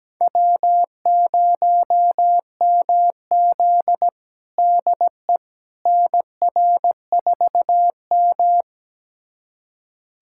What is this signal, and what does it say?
Morse code, heard as a keyed tone, about 17 wpm, 700 Hz. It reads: W0MZ DE NR4M
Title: USA Callsign DE Callsign - 4 Characters - 17wpm